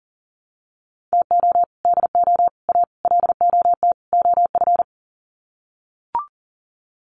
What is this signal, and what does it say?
Morse code, heard as a keyed tone, about 40 wpm, 700 Hz. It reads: TO DO A LOT OF
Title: Sets of 5 Words Encore 40wpm